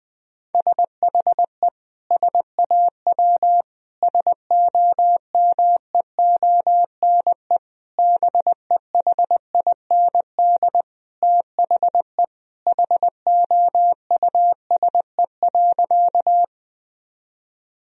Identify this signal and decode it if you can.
Morse code, heard as a keyed tone, about 20 wpm, 700 Hz. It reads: SHE SAW SOMEONE BEHIND THE HOUSE.
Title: Sentences from Top 1000 Words - Review Part 4 - 20wpm